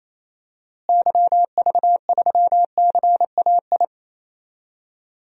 Morse code, heard as a keyed tone, about 28 wpm, 700 Hz. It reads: YV3CAI
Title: DXCC-100 Call Signs - 6 Characters - Repeated 3x - 28wpm